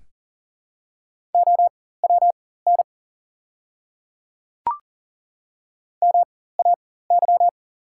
Morse code, heard as a keyed tone, about 38 wpm, spaced as for 18 wpm, 690 Hz. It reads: OWN MAY
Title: 4x Character Spacing - Top 300 Words - 40wpm